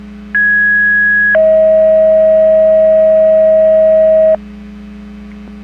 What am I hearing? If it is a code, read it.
Two-tone page, heard as a one-second tone, then a three-second tone, about 1650 then 640 Hz. It is heard from 0.3 to 4.4 s.
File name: Portage 154.385 Paging